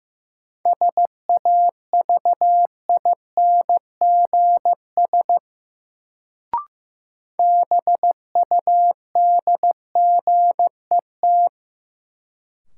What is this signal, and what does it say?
Morse code, heard as a keyed tone, about 15 wpm, 700 Hz. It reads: SAVINGS BUDGET
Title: Sentences from Top 1200 Words - New Word Drill - 15wpm